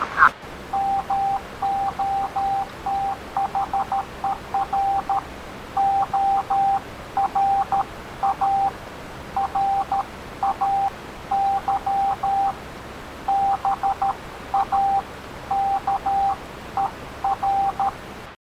Morse code, heard as a keyed tone, about 13 wpm, 790 Hz, with noise. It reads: MOTHER ORA RAY BAKER